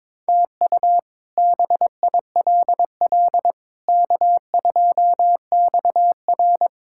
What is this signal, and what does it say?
Morse code, heard as a keyed tone, about 22 wpm, 700 Hz. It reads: TU BILL K2XR
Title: K1USN SST QSO Practice - 22wpm